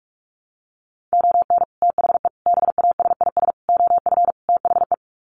Morse code, heard as a keyed tone, about 44 wpm, 700 Hz. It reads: ON THE BASIS OF THE
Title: Sets of 5 Words 45wpm